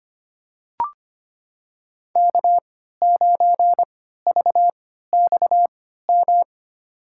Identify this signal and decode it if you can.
Morse code, heard as a keyed tone, about 25 wpm, spaced as for 14 wpm, 700 Hz. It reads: K9VXM
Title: USA Call Signs - 5 Characters - 3x Character Spacing - 25wpm